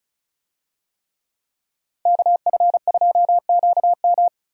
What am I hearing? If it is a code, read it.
Morse code, heard as a keyed tone, about 36 wpm, 700 Hz. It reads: KF2QM
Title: USA Call Signs - 5 Characters - 35wpm